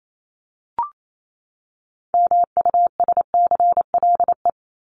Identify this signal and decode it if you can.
Morse code, heard as a keyed tone, about 28 wpm, 700 Hz. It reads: MUSCLE
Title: Top 1251-1500 Words 28wpm